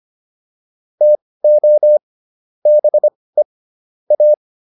Morse code, heard as a keyed tone, about 25 wpm, spaced as for 18 wpm, 600 Hz. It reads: TO BE A
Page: intermediate-practice-files